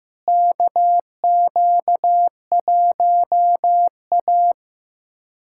Morse code, heard as a keyed tone, about 15 wpm, 700 Hz. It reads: KQ1A
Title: USA Callsign DE Callsign - 4 Characters - 15wpm